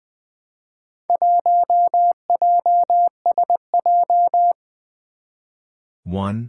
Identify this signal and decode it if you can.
Morse code, heard as a keyed tone, about 20 wpm, 700 Hz. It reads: 1JSJ
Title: Four Challenging Letter-Number Combinations 20wpm